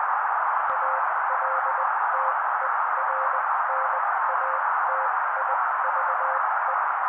Morse code, heard as a keyed tone, about 20 wpm, spaced as for 16 wpm, 530 Hz. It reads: ALTERNATIVE
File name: daily / 16wpm noise